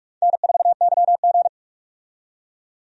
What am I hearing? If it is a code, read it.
Morse code, heard as a keyed tone, about 46 wpm, 700 Hz. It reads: N4YG
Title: USA Call Signs - 4 Characters - 45wpm